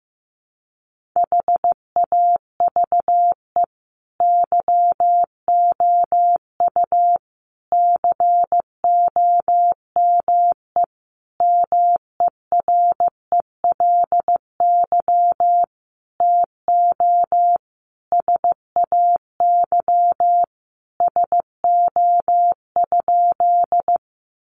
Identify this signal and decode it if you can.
Morse code, heard as a keyed tone, about 15 wpm, 700 Hz. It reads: HAVE YOU COME MERELY TO SAY SO?
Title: Sentences from Top 1400 Words 15wpm